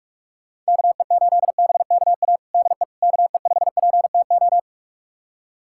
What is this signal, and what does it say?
Morse code, heard as a keyed tone, about 41 wpm, 700 Hz. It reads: KE8BKA DE KE5PTO
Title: USA Callsign DE Callsign - 6 Characters - 45wpm